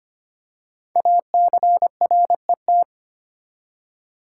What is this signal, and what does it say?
Morse code, heard as a keyed tone, about 25 wpm, 700 Hz. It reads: ACRET
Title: Code Groups - 5 Letters - 25wpm